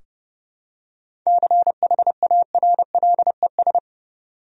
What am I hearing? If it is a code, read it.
Morse code, heard as a keyed tone, about 30 wpm, 700 Hz. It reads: CHARLES
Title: USA Common Names - 30wpm